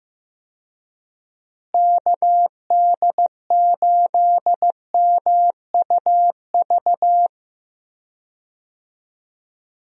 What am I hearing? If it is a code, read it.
Morse code, heard as a keyed tone, about 15 wpm, 700 Hz. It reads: KD8MUV
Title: USA Call Signs - 6 Characters - Repeated 3x - 15wpm